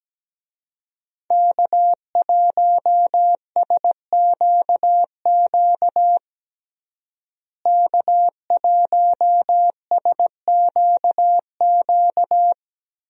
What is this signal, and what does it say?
Morse code, heard as a keyed tone, about 17 wpm, 700 Hz. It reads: K1SQQ K1SQQ
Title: USA Call Signs - 5 Characters - Repeated 3x - 17wpm